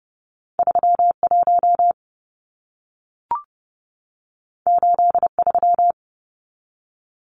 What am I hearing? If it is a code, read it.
Morse code, heard as a keyed tone, about 30 wpm, 700 Hz. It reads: 31 83
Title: Numbers - 2-Digit 30wpm